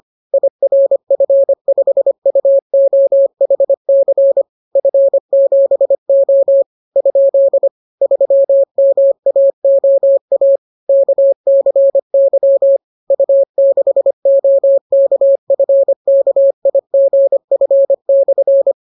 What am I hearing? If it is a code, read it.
Morse code, heard as a keyed tone, about 25 wpm, 550 Hz. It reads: IRF5UOHC F7O ? 3MAOA KCY U6OKFKIGF/